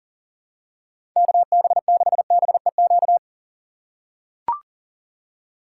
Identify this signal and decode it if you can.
Morse code, heard as a keyed tone, about 40 wpm, 700 Hz. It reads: KB6BEQ
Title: USA Call Signs - 6 Characters - 4x Character Spacing - 40wpm